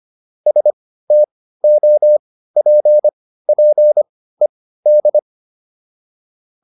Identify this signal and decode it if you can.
Morse code, heard as a keyed tone, about 25 wpm, spaced as for 15 wpm, 600 Hz. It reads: STOPPED